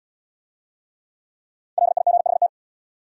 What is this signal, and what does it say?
Morse code, heard as a keyed tone, about 71 wpm, 700 Hz. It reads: LIGHT